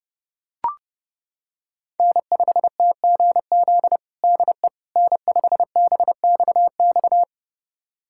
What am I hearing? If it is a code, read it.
Morse code, heard as a keyed tone, about 32 wpm, 700 Hz. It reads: N5TGZ DE N5BXX
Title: USA Callsign DE Callsign - 5 Characters - 30wpm